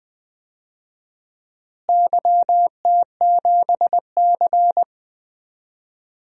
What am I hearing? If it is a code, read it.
Morse code, heard as a keyed tone, about 20 wpm, 700 Hz. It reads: YT7C
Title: DXCC-100 Call Signs - 4 Characters - Repeated 3x - 20wpm